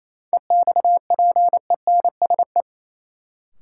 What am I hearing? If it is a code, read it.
Morse code, heard as a keyed tone, about 28 wpm, 700 Hz. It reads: EXPENSE
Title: Sets of 4 Words - New Word Drill - 28wpm